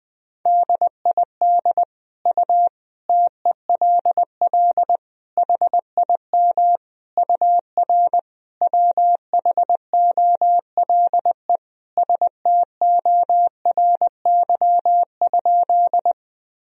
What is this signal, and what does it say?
Morse code, heard as a keyed tone, about 20 wpm, 700 Hz. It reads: DID U TELL HIM UR WHOLE STORY?
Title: Sentences from Top 400 Words and Abbreviations 20wpm